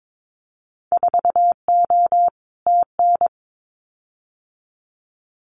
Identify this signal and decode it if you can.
Morse code, heard as a keyed tone, about 22 wpm, 700 Hz. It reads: 4O TN
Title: Winter Field Day Exchanges Encore 22wpm